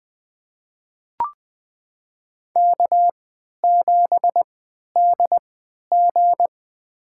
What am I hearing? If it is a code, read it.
Morse code, heard as a keyed tone, about 20 wpm, spaced as for 11 wpm, 700 Hz. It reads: K7DG
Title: USA Call Signs - 4 Characters - 3x Character Spacing - 20wpm